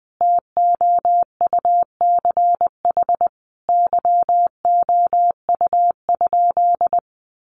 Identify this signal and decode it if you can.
Morse code, heard as a keyed tone, about 20 wpm, 700 Hz. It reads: TOUCH YOU?
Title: Sentences from Top 1200 Words 20wpm